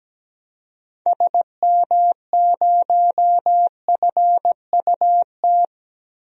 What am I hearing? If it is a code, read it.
Morse code, heard as a keyed tone, about 17 wpm, 700 Hz. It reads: SM0FUT
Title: DXCC-100 Call Signs - 6 Characters - Repeated 2x - 17wpm